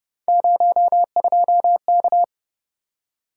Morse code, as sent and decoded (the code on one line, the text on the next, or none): ----- ..--- -.-
02K